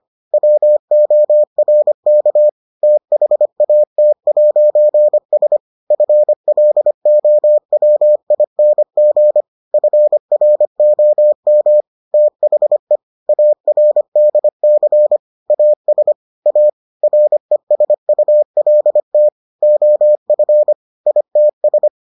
Morse code, heard as a keyed tone, about 25 wpm, 600 Hz. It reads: WORK THAT'S FLOWING FROM THE ARDC AS A RESULT OF ITS